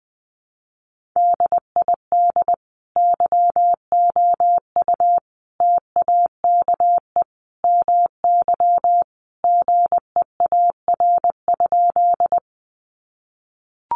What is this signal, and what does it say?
Morse code, heard as a keyed tone, about 20 wpm, 700 Hz. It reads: DID YOU TAKE MY GEAR?
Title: Sentences from Top 1400 Words 20wpm